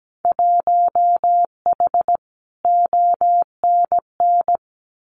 Morse code, heard as a keyed tone, about 17 wpm, 700 Hz. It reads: 1H ONN
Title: Winter Field Day Exchanges 17wpm